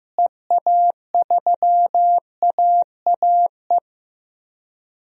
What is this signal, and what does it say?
Morse code, heard as a keyed tone, about 15 wpm, 700 Hz. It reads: EA3AAE
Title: DXCC-100 Call Signs - 6 Characters - Repeated 3x - 15wpm